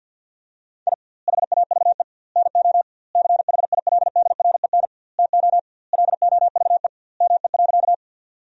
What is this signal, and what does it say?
Morse code, heard as a keyed tone, about 50 wpm, 700 Hz. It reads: I HAVE NO CHILDREN TO LOVE ME.
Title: 2x Word-Spacing - Sentences from Top 400 Words - 50wpm